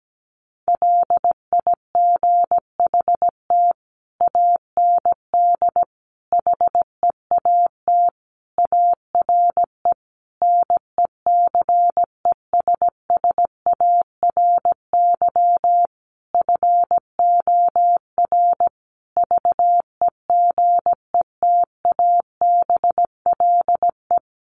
Morse code, heard as a keyed tone, about 17 wpm, 700 Hz. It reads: LIGHT AND HEAT ARE NECESSARY FOR VEGETABLE
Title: Sentences from Top 1300 Words 17wpm